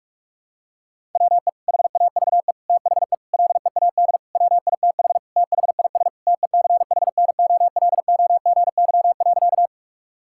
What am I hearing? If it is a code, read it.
Morse code, heard as a keyed tone, about 46 wpm, 690 Hz. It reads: WE HAVE THE LEAD WITH THIS TECHNOLOGY.
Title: Sentences from Top 1000 Words - Review Part 5 - 45wpm